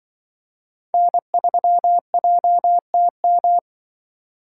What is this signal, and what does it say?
Morse code, heard as a keyed tone, about 24 wpm, 700 Hz. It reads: N3JTM
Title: Speed Racing - USA 1x3 Call Signs 20wpm